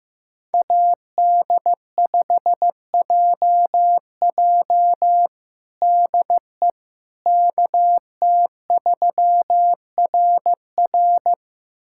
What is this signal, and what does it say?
Morse code, heard as a keyed tone, about 16 wpm, 700 Hz. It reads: AD5JJ DE KT3RR
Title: USA Callsign DE Callsign - 5 Characters - 15wpm